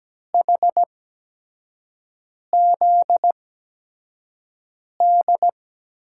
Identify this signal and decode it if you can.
Morse code, heard as a keyed tone, about 17 wpm, spaced as for 5 wpm, 700 Hz. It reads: HZD
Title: USA Call Signs - 8x Character Spacing - 17wpm